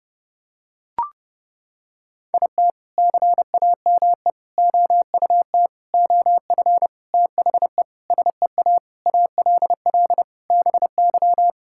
Morse code, heard as a keyed tone, about 28 wpm, 700 Hz. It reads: IT CAME OUT OF THE SEA ALL BY
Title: Sentences from Top 1000 Words - Review Part 5 - 30wpm